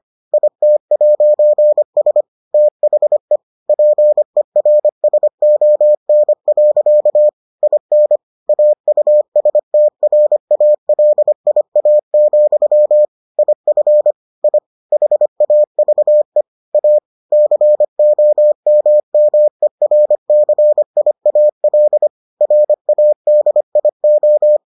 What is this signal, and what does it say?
Morse code, heard as a keyed tone, about 25 wpm, 600 Hz. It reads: IT'S THE PERSON. IN AUSTRALIA, IF I HAVE A COMMERCIAL RADIO